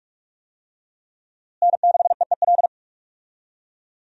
Morse code, heard as a keyed tone, about 42 wpm, 700 Hz. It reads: N6EEL
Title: USA Call Signs - 5 Characters - Repeated 2x - 45wpm